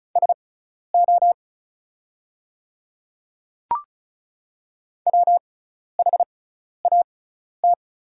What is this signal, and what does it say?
Morse code, heard as a keyed tone, about 34 wpm, spaced as for 12 wpm, 700 Hz. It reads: SO WHAT
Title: 6x Character Spacing - Top 200 Words - 35wpm